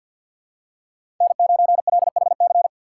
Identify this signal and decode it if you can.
Morse code, heard as a keyed tone, about 47 wpm, 700 Hz. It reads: N9LHC
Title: USA Call Signs - 5 Characters - 6x Character Spacing - 50wpm